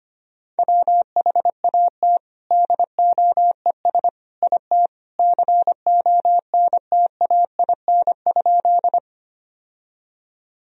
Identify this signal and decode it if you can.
Morse code, heard as a keyed tone, about 25 wpm, 700 Hz. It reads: WHAT DOES IT CONTAIN?
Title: Sentences from Top 1000 Words - Review Part 2 - 25wpm